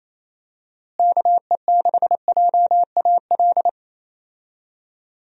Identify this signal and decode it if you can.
Morse code, heard as a keyed tone, about 27 wpm, 700 Hz. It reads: KE6JAL
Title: USA Call Signs - 6 Characters - Repeated 2x - 28wpm